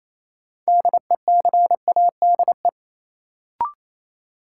Morse code, heard as a keyed tone, about 28 wpm, 700 Hz. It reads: DECADE
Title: Sets of 5 Words - New Word Drill - 28wpm